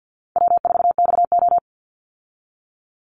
Morse code, heard as a keyed tone, about 50 wpm, 700 Hz. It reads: W4XO
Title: Easy Call Signs Repeated 2x - 50wpm